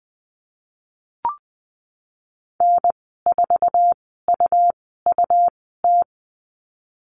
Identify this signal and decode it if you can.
Morse code, heard as a keyed tone, about 20 wpm, spaced as for 15 wpm, 700 Hz. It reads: N4UUT
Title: 154 Lesson 25g - Letter p - Call Signs 20wpm